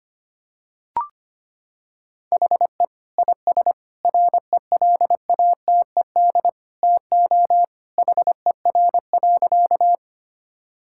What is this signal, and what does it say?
Morse code, heard as a keyed tone, about 25 wpm, 700 Hz. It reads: HE IS RELATED TO HER.
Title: Sentences from Top 1000 Words - Review Part 5 - 25wpm